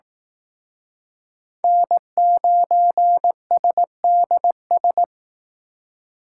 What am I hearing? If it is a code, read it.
Morse code, heard as a keyed tone, about 18 wpm, 700 Hz. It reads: N9SDS
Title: Speed Racing - USA 1x3 Call Signs 15wpm